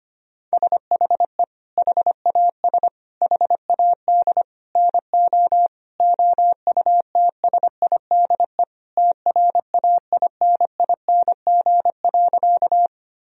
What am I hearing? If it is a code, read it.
Morse code, heard as a keyed tone, about 25 wpm, 700 Hz. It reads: SHE HAS HAD NO OUTSIDE TRAINING.
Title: Sentences from Top 1000 Words - Review Part 4 - 25wpm